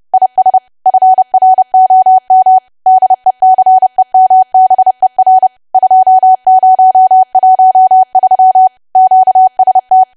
Morse code, heard as a keyed tone, about 30 wpm, 750 Hz. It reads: IS FROM DECEMBER 2013 QST
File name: cwa / W1AW-Level3 / 30WPM